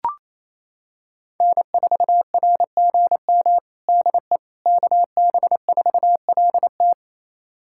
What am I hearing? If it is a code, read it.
Morse code, heard as a keyed tone, about 28 wpm, 700 Hz. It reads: N4RGM DE KB4LT
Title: USA Callsign DE Callsign - 5 Characters - 28wpm